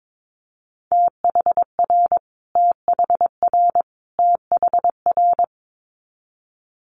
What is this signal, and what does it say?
Morse code, heard as a keyed tone, about 22 wpm, 700 Hz. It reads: THR THR THR
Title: Multi-letter Phonemes Repeated 3x 22wpm